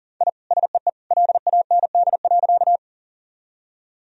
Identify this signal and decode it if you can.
Morse code, heard as a keyed tone, about 40 wpm, 700 Hz. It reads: I SEE LAND.
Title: Sentences from Top 1000 Words - Review Part 1 - 40wpm